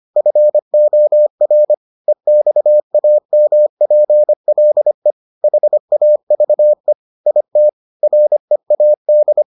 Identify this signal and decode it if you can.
Morse code, heard as a keyed tone, about 25 wpm, 600 Hz. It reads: FOR EXAMPLE HAVE IT READ